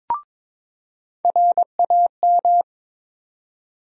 Morse code, heard as a keyed tone, about 22 wpm, 700 Hz. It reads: RAM
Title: USA Call Sign Postfixes - 3 Letters 22wpm